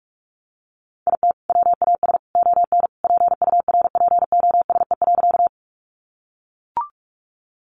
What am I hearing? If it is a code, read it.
Morse code, heard as a keyed tone, about 44 wpm, 700 Hz. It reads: IT WAS ON PURPOSE.
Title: Sentences from Top 800 Words 45wpm